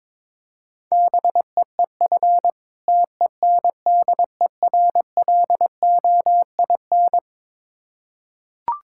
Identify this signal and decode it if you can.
Morse code, heard as a keyed tone, about 22 wpm, 690 Hz. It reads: BEEF TENDERLOIN
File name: Difficult Words with Hints - 22wpm